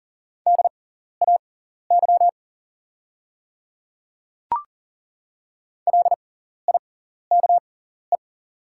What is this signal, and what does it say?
Morse code, heard as a keyed tone, about 40 wpm, spaced as for 14 wpm, 700 Hz. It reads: DAY LIKE
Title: 6x Character Spacing - Top 300 Words - 40wpm